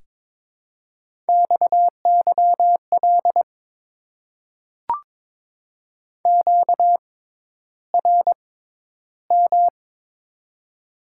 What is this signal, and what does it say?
Morse code, heard as a keyed tone, about 22 wpm, spaced as for 11 wpm, 700 Hz. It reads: XYL QRM
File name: Top 90 QSO Elements - 6x Character Spacing - 22wpm